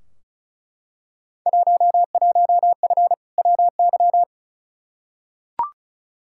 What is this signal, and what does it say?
Morse code, heard as a keyed tone, about 36 wpm, 700 Hz. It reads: 11F WY
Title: Field Day Exchanges Encore - 35wpm